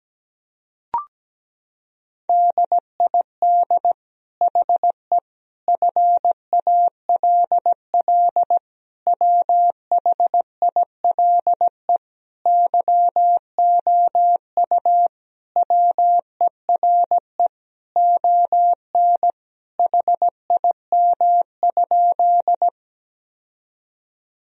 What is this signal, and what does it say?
Morse code, heard as a keyed tone, about 17 wpm, 700 Hz. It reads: DID HE FALL WHILE YOU WERE ON HIM?
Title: Sentences from Top 1000 Words - Review Part 2 - 17wpm